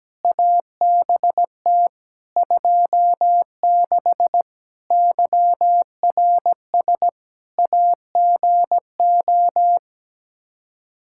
Morse code, heard as a keyed tone, about 17 wpm, 700 Hz. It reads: ABT 26 YRS AGO
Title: Practice QSO - Exchange 3 - More Info - 17wpm